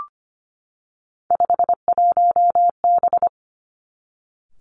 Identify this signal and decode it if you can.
Morse code, heard as a keyed tone, about 25 wpm, 700 Hz. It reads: 51B
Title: Three Challenging Letter-Number Combinations 25wpm